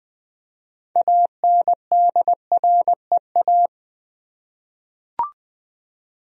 Morse code, heard as a keyed tone, about 21 wpm, 700 Hz. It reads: ANDREA
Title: USA Common Names - 20wpm